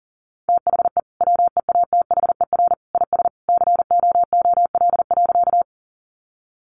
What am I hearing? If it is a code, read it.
Morse code, heard as a keyed tone, about 40 wpm, 700 Hz. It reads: THE WEATHER IS COOL.
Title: Sentences from Top 1200 Words 40wpm